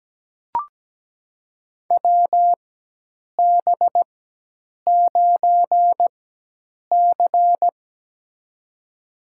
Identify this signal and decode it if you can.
Morse code, heard as a keyed tone, about 17 wpm, spaced as for 8 wpm, 700 Hz. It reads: WB9C
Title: USA Call Signs - 4 Characters - 4x Character Spacing - 17wpm